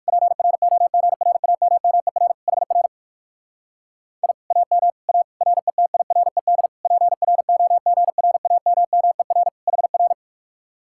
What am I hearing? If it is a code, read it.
Morse code, heard as a keyed tone, about 58 wpm, 700 Hz. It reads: PROGRAMMER HR I AM A RETIRED PROGRAMMER HR